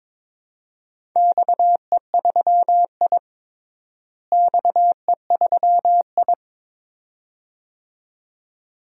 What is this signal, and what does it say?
Morse code, heard as a keyed tone, about 22 wpm, 700 Hz. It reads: XE3I XE3I
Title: DXCC-100 Call Signs - 4 Characters - Repeated 2x - 22wpm